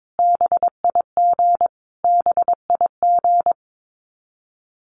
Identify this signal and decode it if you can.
Morse code, heard as a keyed tone, about 22 wpm, 700 Hz. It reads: BIG BIG
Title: Top 300 Words Repeated 22wpm